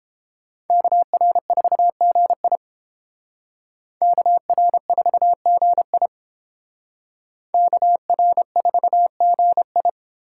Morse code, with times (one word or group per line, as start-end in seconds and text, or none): 0.7-2.6 KR4GI
4.0-6.1 KR4GI
7.5-9.9 KR4GI